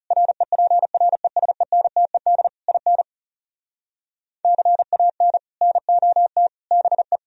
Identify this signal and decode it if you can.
Morse code, heard as a keyed tone, about 40 wpm, 700 Hz. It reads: REPRESENTED IN CAN NOT BE